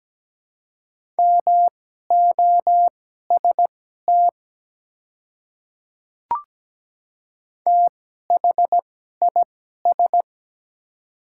Morse code, heard as a keyed tone, about 17 wpm, spaced as for 12 wpm, 690 Hz. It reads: MOST THIS
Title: 2x Character Spacing - Top 100 Words - 17wpm